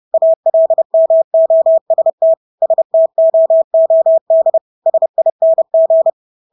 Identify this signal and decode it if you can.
Morse code, heard as a keyed tone, about 30 wpm, 650 Hz. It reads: ALMOST STOOD SING
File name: Engelske ord 30-30 wpm